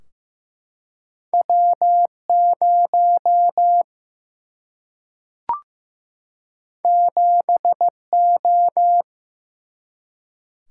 Morse code, heard as a keyed tone, about 15 wpm, 700 Hz. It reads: W 0 7 O
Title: Single-Letter Single-Number 15wpm